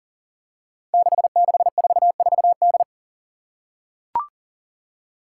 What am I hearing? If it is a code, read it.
Morse code, heard as a keyed tone, about 40 wpm, 700 Hz. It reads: 6644D
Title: Five Challenging Letter-Number Combinations 40wpm